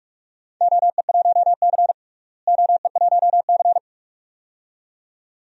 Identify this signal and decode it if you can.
Morse code, heard as a keyed tone, about 44 wpm, 700 Hz. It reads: OE1C OE1C
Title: DXCC-100 Call Signs - 4 Characters - Repeated 2x - 45wpm